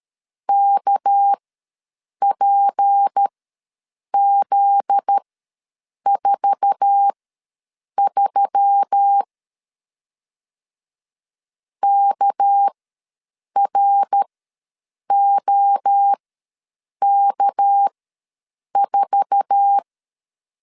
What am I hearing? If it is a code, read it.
Morse code, heard as a keyed tone, about 13 wpm, spaced as for 7 wpm, 800 Hz. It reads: KPZ43 KROK4
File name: LEZIONI / 8 YJ45 / 8 YJ4 ws 12wpm 3249